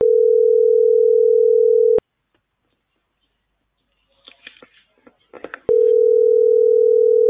Ringback tone (North America), 440 + 480 Hz.